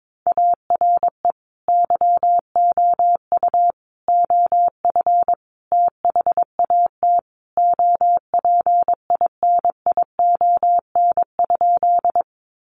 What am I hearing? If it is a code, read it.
Morse code, heard as a keyed tone, about 22 wpm, 700 Hz. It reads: ARE YOU OF THAT OPINION?
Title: Sentences from Top 1200 Words 22wpm